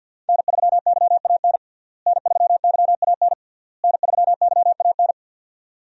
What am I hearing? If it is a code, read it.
Morse code, heard as a keyed tone, about 51 wpm, 700 Hz. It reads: N3YAN N3YAN N3YAN
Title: USA Call Signs - 5 Characters - Repeated 3x - 50wpm